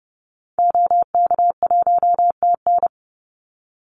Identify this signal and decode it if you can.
Morse code, heard as a keyed tone, about 30 wpm, 700 Hz. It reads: OK1TN
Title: Call Signs Repeated 2x - 30wpm